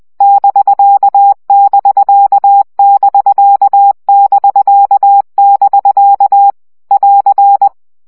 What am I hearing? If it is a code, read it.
Morse code, heard as a keyed tone, about 20 wpm, 800 Hz. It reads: <BK><BK><BK><BK><BK> <AR>
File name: CW gyakorlas bk 17 baud